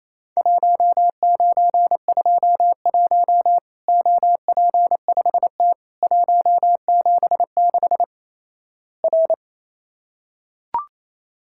Morse code, heard as a keyed tone, about 28 wpm, 700 Hz. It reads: 1921 OP5T 176 R
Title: Worked All Europe DX Contest - QTC Exchange - 28wpm